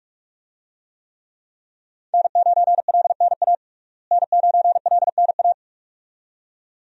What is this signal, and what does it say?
Morse code, heard as a keyed tone, about 46 wpm, 700 Hz. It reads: N9LNA N9LNA
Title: USA Call Signs - 5 Characters - Repeated 2x - 45wpm